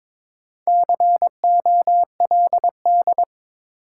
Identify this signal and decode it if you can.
Morse code, heard as a keyed tone, about 22 wpm, 700 Hz. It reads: COLD
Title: Sets of 5 Words - New Word Drill - 22wpm